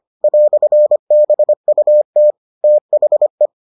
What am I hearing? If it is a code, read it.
Morse code, heard as a keyed tone, about 25 wpm, 600 Hz. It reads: "BUT THE